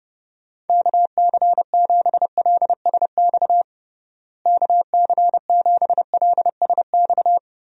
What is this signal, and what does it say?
Morse code, heard as a keyed tone, about 31 wpm, 700 Hz. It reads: KC7LSX KC7LSX
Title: USA Call Signs - 6 Characters - Repeated 3x - 30wpm